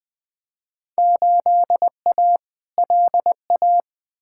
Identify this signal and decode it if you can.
Morse code, heard as a keyed tone, about 20 wpm, 700 Hz. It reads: 8A LA
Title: Field Day Exchanges Encore - 20wpm